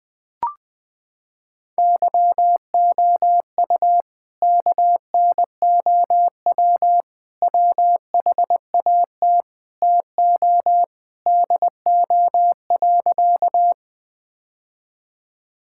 Morse code, heard as a keyed tone, about 20 wpm, 700 Hz. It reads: YOU KNOW WHAT TO DO.
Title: Sentences with 3-Word Phrases - 20wpm